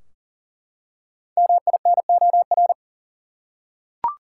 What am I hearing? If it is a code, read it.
Morse code, heard as a keyed tone, about 40 wpm, 700 Hz. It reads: MINOR
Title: Five Letter Words - 40wpm